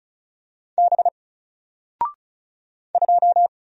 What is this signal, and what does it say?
Morse code, heard as a keyed tone, about 35 wpm, 700 Hz. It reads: B 2
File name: Single-Letter-Number-35wpm